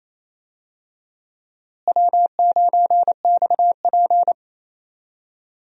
Morse code, heard as a keyed tone, about 28 wpm, 700 Hz. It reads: W9XP
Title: USA Call Signs - 4 Characters - Repeated 3x - 28wpm